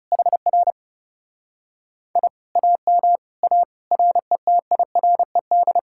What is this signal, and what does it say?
Morse code, heard as a keyed tone, about 40 wpm, 700 Hz. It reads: HR I AM A RETIRED